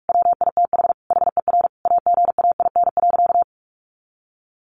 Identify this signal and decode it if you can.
Morse code, heard as a keyed tone, about 44 wpm, 700 Hz. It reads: WITH HER AGAIN.
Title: Sentences from Top 1500 Words 45wpm